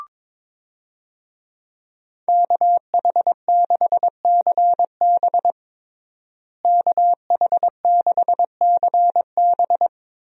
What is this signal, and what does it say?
Morse code, heard as a keyed tone, about 22 wpm, 700 Hz. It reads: KH6CB KH6CB
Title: USA Call Signs - 5 Characters - Repeated 2x - 22wpm